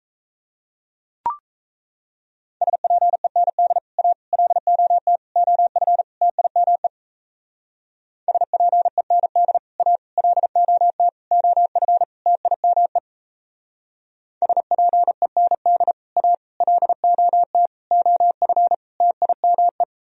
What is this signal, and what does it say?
Morse code, heard as a keyed tone, about 34 wpm, 700 Hz. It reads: SPEND A LOT OF TIME SPEND A LOT OF TIME SPEND A LOT OF TIME
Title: Speed-Racing - Sets of 5 Words 28wpm